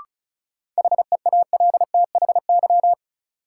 Morse code, heard as a keyed tone, about 36 wpm, 700 Hz. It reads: HEALTHY
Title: Sentences from Top 1000 Words - Review Part 5 - New Word Drill - 35wpm